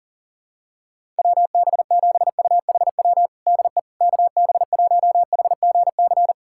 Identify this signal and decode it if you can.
Morse code, heard as a keyed tone, about 40 wpm, 700 Hz. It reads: WB7UHW DE KB1HGC
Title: USA Callsign DE Callsign 40wpm